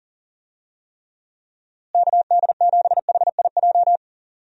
Morse code, heard as a keyed tone, about 40 wpm, 700 Hz. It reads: KD7HIJ
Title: USA Call Signs - 6 Characters - Repeated 2x - 40wpm